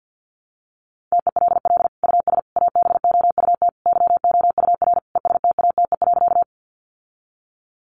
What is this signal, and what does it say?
Morse code, heard as a keyed tone, about 50 wpm, 700 Hz. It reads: TELL US ABOUT YOUR ESTATE.
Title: Sentences from Top 1200 Words 50wpm